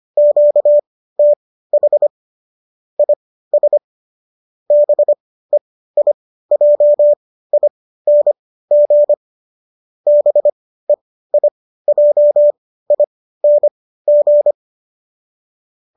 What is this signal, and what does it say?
Morse code, heard as a keyed tone, about 25 wpm, spaced as for 15 wpm, 590 Hz. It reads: QTH IS BEIJING BEIJING
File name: qso203 15